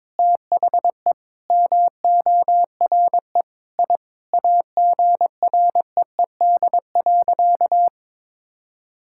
Morse code, heard as a keyed tone, about 22 wpm, 700 Hz. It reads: THE MORE I AGREED.
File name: practice / 22wpm / Sentences with 8-Word Phrases - 22wpm